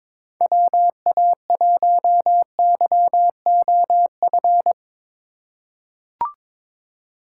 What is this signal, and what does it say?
Morse code, heard as a keyed tone, about 22 wpm, 700 Hz. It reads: WA1YOF
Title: USA Call Signs - 6 Characters - 3x Character Spacing - 22wpm